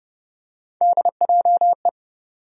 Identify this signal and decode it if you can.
Morse code, heard as a keyed tone, about 30 wpm, 700 Hz. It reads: DJE
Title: USA Call Sign Postfixes - 3 Letters 30wpm